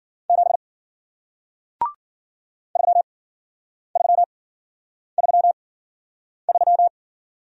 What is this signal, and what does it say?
Morse code, heard as a keyed tone, about 56 wpm, spaced as for 9 wpm, 700 Hz. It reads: 7 3333